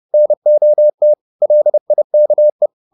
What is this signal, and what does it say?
Morse code, heard as a keyed tone, about 30 wpm, 600 Hz. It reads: NOT LIKE